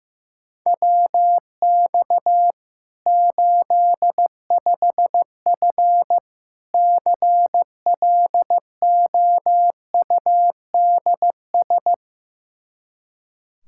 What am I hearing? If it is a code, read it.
Morse code, heard as a keyed tone, about 15 wpm, 700 Hz. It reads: WX 85F CLOUDS
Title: Practice QSO - Exchange 2 - Info - 15wpm